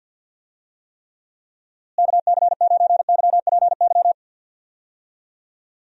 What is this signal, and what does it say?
Morse code, heard as a keyed tone, about 51 wpm, 700 Hz. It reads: KC9YPY
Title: USA Call Signs - 6 Characters - Repeated 2x - 50wpm